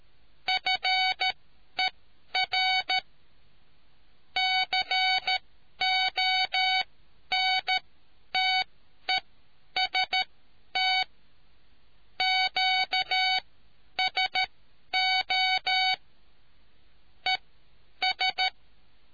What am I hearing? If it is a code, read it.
Morse code, heard as a keyed tone, about 13 wpm, spaced as for 10 wpm, 750 Hz. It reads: FER CONTEST QSO ES